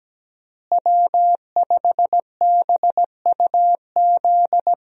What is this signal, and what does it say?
Morse code, heard as a keyed tone, about 17 wpm, 700 Hz. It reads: W5BUZ
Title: USA Call Signs - 5 Characters - Repeated 2x - 17wpm